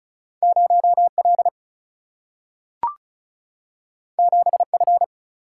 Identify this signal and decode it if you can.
Morse code, heard as a keyed tone, about 36 wpm, 700 Hz. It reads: 0 L 7 F
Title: Single-Letter Single-Number 35wpm